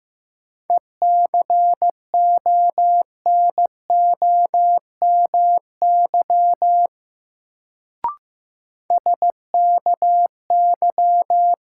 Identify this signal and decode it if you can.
Morse code, heard as a keyed tone, about 15 wpm, 700 Hz. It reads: ECONOMY SKY
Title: Sentences from Top 1200 Words - New Word Drill Alt - 15wpm